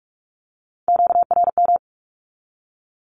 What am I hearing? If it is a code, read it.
Morse code, heard as a keyed tone, about 44 wpm, 700 Hz. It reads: QRM
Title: Q-Codes 45wpm